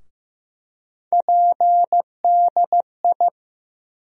Morse code, heard as a keyed tone, about 15 wpm, 700 Hz. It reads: PDI
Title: USA Call Sign Postfixes - 3 Letters 15wpm